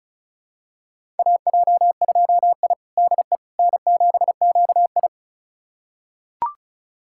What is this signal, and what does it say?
Morse code, heard as a keyed tone, about 36 wpm, 700 Hz. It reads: AJ2I DE N7QI
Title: USA Callsign DE Callsign - 4 Characters - 35wpm